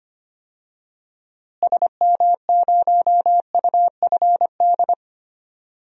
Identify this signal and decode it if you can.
Morse code, heard as a keyed tone, about 25 wpm, 700 Hz. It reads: SM0UFD
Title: DXCC-100 Call Signs - 6 Characters - Repeated 3x - 25wpm